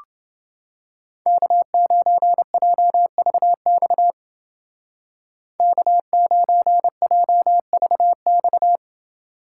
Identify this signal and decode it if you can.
Morse code, heard as a keyed tone, about 28 wpm, 700 Hz. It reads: K9JVX K9JVX